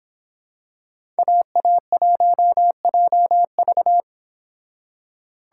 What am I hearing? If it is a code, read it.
Morse code, heard as a keyed tone, about 25 wpm, 700 Hz. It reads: AA1JV